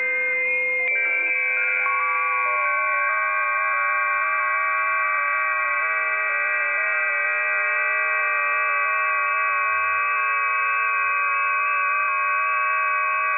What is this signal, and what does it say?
FT8.
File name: tests / websdr test1